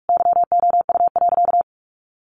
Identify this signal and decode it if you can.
Morse code, heard as a keyed tone, about 46 wpm, 700 Hz. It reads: YOU.
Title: Sentences from Top 400 Words 45wpm